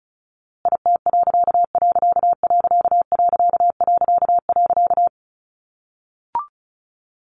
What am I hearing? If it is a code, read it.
Morse code, heard as a keyed tone, about 34 wpm, 700 Hz. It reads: IT......
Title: A Sound Of Thunder 35wpm